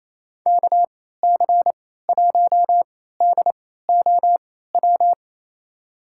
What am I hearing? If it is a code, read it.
Morse code, heard as a keyed tone, about 27 wpm, spaced as for 16 wpm, 690 Hz. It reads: KC1DOW
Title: USA Call Signs - 3x Character Spacing - 28wpm